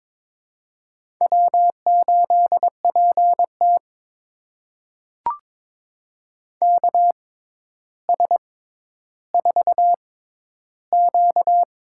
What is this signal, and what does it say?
Morse code, heard as a keyed tone, about 22 wpm, spaced as for 11 wpm, 700 Hz. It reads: W8PT KS4Q
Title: USA Call Signs - 4 Characters - 6x Character Spacing - 22wpm